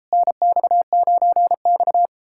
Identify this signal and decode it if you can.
Morse code, heard as a keyed tone, about 38 wpm, 700 Hz. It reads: NX9X